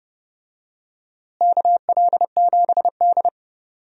Morse code, heard as a keyed tone, about 32 wpm, 700 Hz. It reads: KL7D
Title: DXCC-100 Call Signs - 4 Characters - Repeated 3x - 30wpm